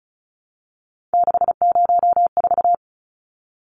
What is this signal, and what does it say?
Morse code, heard as a keyed tone, about 34 wpm, 700 Hz. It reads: 604
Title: Numbers - 3-Digit 35wpm